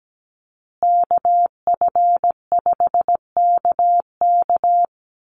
Morse code, heard as a keyed tone, about 17 wpm, 700 Hz. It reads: KF5KK
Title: USA Call Signs Repeated 3x 17wpm